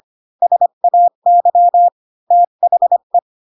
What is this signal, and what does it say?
Morse code, heard as a keyed tone, about 25 wpm, 700 Hz. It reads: SAY THE
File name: Pinocchio / Pinocchio ch1 20 25 ik1hge